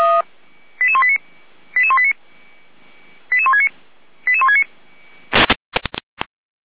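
Five-tone selective call (ZVEI), heard as quick runs of five tones, tones of about 70 ms. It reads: 88189 88189 88169 88169
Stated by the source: ZVEI five-tone alert received by SDR.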